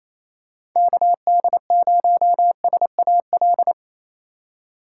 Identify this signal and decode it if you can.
Morse code, heard as a keyed tone, about 28 wpm, 700 Hz. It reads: KD0SAL
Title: USA Call Signs - 6 Characters - 2x Character Spacing - 28wpm